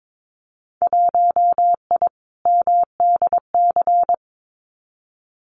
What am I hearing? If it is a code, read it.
Morse code, heard as a keyed tone, about 22 wpm, 700 Hz. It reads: 1I MDC
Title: Winter Field Day Exchanges 22wpm